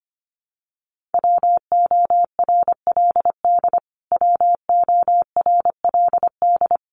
Morse code, heard as a keyed tone, about 25 wpm, 700 Hz. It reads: WORLD WORLD
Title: Top 300 Words Repeated 25wpm